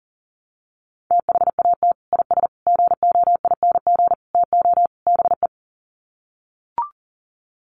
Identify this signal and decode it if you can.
Morse code, heard as a keyed tone, about 39 wpm, 700 Hz. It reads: THAT IS GOING TO BE
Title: Sets of 5 Words Encore2 40wpm